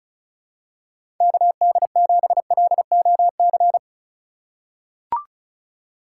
Morse code, heard as a keyed tone, about 34 wpm, 700 Hz. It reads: KD7LOC
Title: USA Call Signs - 6 Characters - 2x Character Spacing - 35wpm